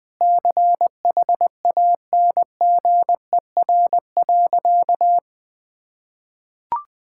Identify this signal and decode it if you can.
Morse code, heard as a keyed tone, about 20 wpm, 700 Hz. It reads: CHANGER.
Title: Sentences with 6-Word Phrases - 20wpm